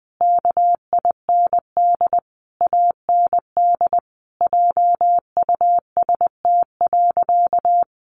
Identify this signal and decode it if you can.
Morse code, heard as a keyed tone, about 20 wpm, 700 Hz. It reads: KIND AND JUST.
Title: Sentences from Top 800 Words 20wpm